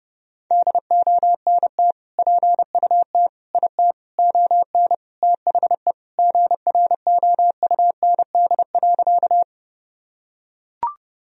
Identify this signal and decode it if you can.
Morse code, heard as a keyed tone, about 32 wpm, 700 Hz. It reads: DONT PUT IT ON THE GROUND.
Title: Sentences from Top 1000 Words - Review Part 2 - 30wpm